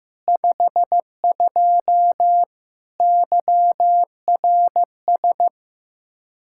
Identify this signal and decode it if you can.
Morse code, heard as a keyed tone, about 15 wpm, 700 Hz. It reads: 52 YRS
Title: Practice QSO - Exchange 3 - More Info - 15wpm